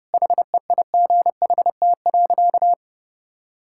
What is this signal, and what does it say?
Morse code, heard as a keyed tone, about 30 wpm, 700 Hz. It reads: HEIGHT.
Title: Sentences from Top 1600 Words 30wpm